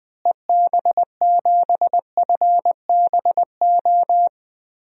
Morse code, heard as a keyed tone, about 20 wpm, 690 Hz. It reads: EB7FBO
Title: DXCC-100 Call Signs - 6 Characters - Repeated 2x - 20wpm